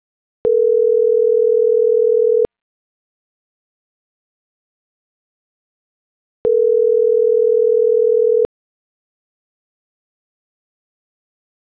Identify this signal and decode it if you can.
Ringback tone (North America), 440 + 480 Hz.